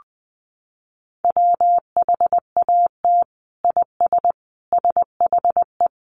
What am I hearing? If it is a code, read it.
Morse code, heard as a keyed tone, about 20 wpm, 700 Hz. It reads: WHAT IS SHE
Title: Sentences from Top 800 Words 20wpm